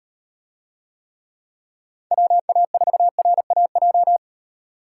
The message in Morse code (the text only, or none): WA4RAJ